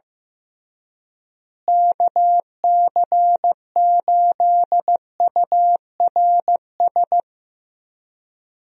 Morse code, heard as a keyed tone, about 15 wpm, 700 Hz. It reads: KC8URS
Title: USA Call Signs - 6 Characters - Repeated 3x - 15wpm